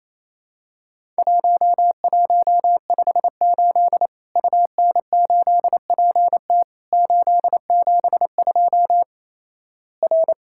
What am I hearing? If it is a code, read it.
Morse code, heard as a keyed tone, about 28 wpm, 700 Hz. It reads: 1158 UN8PT 872 R
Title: Worked All Europe DX Contest - QTC Exchange - 28wpm